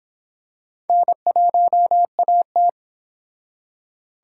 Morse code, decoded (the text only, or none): N1AT